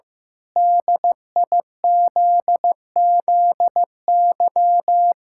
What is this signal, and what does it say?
Morse code, heard as a keyed tone, about 16 wpm, 700 Hz. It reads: DIZZY
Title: Pangram Challenge v2 - 15wpm